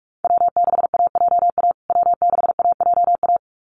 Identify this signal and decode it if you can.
Morse code, heard as a keyed tone, about 44 wpm, 690 Hz. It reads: W6AJA W6AJA
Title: USA Call Signs Repeated 2x 45wpm